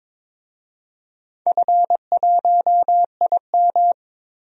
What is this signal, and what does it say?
Morse code, heard as a keyed tone, about 22 wpm, 700 Hz. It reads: F1IM
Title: DXCC-100 Call Signs - 4 Characters - Repeated 3x - 22wpm